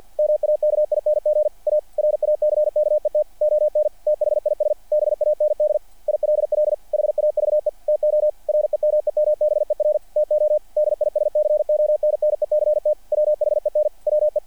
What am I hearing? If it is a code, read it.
Morse code, heard as a keyed tone, about 48 wpm, 600 Hz, with radio noise. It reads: MAKING A RACKET ON THIS BAND ILL HAVE TO REMEMBER TO DISCONNECT WHEN WE
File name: archived / CRSnet